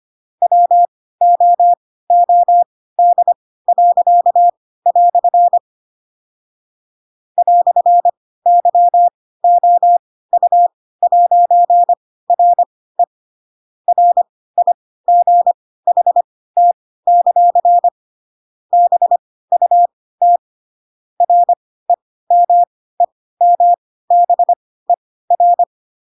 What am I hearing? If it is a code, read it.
Morse code, heard as a keyed tone, about 25 wpm, spaced as for 16 wpm, 700 Hz. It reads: WOOD." "YOU'RE RIGHT; BUT REMEMBER
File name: Pinocchio / Pinocchio ch2 10 25 ik1hge